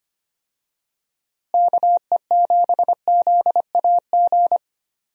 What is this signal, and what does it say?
Morse code, heard as a keyed tone, about 25 wpm, 700 Hz. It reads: KE7ZAG
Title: USA Call Signs - 6 Characters - Repeated 3x - 25wpm